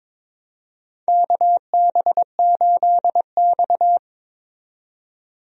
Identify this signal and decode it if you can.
Morse code, heard as a keyed tone, about 22 wpm, 700 Hz. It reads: KB8X
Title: USA Call Signs - 4 Characters - 22wpm